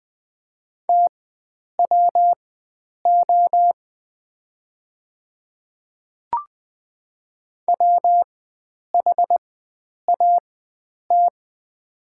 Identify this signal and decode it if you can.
Morse code, heard as a keyed tone, about 20 wpm, spaced as for 9 wpm, 700 Hz. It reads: TWO WHAT
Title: 4x Character Spacing - Top 100 Words - 20wpm